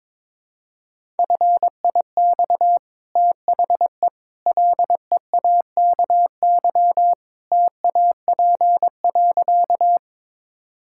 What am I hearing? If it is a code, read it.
Morse code, heard as a keyed tone, about 22 wpm, 700 Hz. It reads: FIX THE LEAKY TAP.
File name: practice / 22wpm / Easy Sentences with Hints - 22wpm